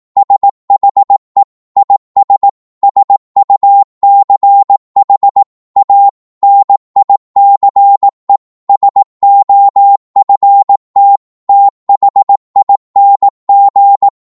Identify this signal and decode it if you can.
Morse code, heard as a keyed tone, about 18 wpm, 800 Hz. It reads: SHE IS SUCH A NICE SOFT THING
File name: Morse / psg-18wpm